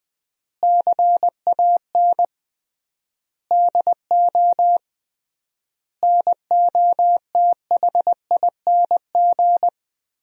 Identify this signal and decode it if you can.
Morse code, heard as a keyed tone, about 20 wpm, 700 Hz. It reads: CAN DO NOTHING
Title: 3x Word-Spacing - Sentences from Top 500 Words - 20wpm